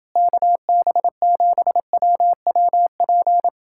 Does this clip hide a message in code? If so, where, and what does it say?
Yes: Morse code from 0.2 to 3.5 s, heard as a keyed tone, about 30 wpm, 700 Hz. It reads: KB7WWP